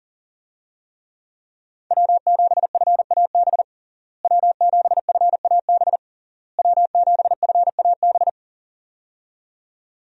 Morse code, heard as a keyed tone, about 37 wpm, 700 Hz. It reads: W7FAB W7FAB W7FAB
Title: USA Call Signs - 5 Characters - Repeated 3x - 40wpm